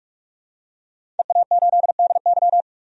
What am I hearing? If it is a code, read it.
Morse code, heard as a keyed tone, about 44 wpm, 700 Hz. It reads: EA8DY
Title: DXCC-100 Call Signs - 5 Characters - 45wpm